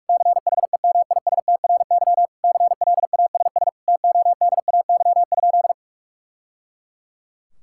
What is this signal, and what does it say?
Morse code, heard as a keyed tone, about 46 wpm, 700 Hz. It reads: CHEMISTRY CLASS TODAY?
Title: Sentences from Top 1000 Words - Review Part 6 - 45wpm